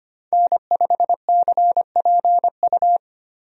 Morse code, heard as a keyed tone, about 25 wpm, 700 Hz. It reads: N5CPU
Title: USA Call Signs - 5 Characters - 25wpm